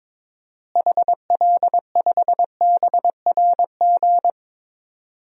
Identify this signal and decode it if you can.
Morse code, heard as a keyed tone, about 22 wpm, 700 Hz. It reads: HL5BRG
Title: DXCC-100 Call Signs - 6 Characters - Repeated 3x - 22wpm